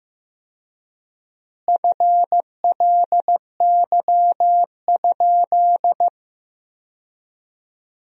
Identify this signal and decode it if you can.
Morse code, heard as a keyed tone, about 15 wpm, 700 Hz. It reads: FLY?
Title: 4x Word-Spacing - Sentences from Top 500 Words - 15wpm